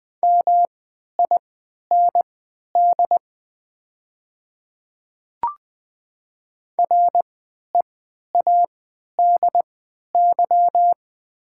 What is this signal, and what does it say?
Morse code, heard as a keyed tone, about 20 wpm, spaced as for 11 wpm, 700 Hz. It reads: MIND READY
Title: 3x Character Spacing - Top 500 Words - 20wpm